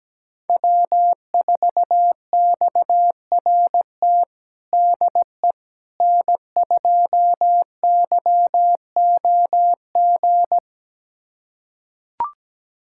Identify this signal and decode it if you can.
Morse code, heard as a keyed tone, about 17 wpm, 700 Hz. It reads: W4XRT DE N2YOG
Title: USA Callsign DE Callsign 17wpm